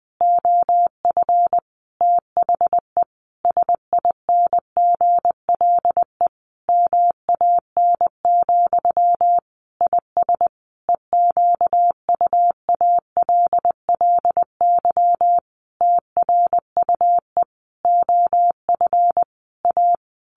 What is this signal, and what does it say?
Morse code, heard as a keyed tone, about 20 wpm, 700 Hz. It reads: OF THE SINGLE MAN, IS EQUALLY TRUE OF A
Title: Sentences from Top 1400 Words 20wpm